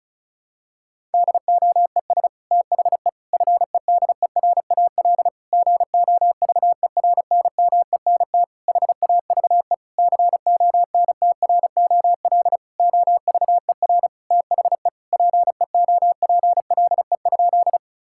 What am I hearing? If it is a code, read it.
Morse code, heard as a keyed tone, about 36 wpm, 700 Hz. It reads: DOES THE FEDERAL GOVERNMENT HAVE CONTROL OVER THE PEOPLE?
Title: Sentences from Top 1000 Words - Review Part 5 - 35wpm